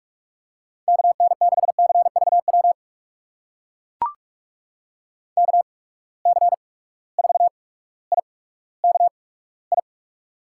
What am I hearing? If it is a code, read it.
Morse code, heard as a keyed tone, about 44 wpm, spaced as for 19 wpm, 700 Hz. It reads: KN6CVW KC4IKI
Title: USA Call Signs - 6 Characters - 8x Character Spacing - 45wpm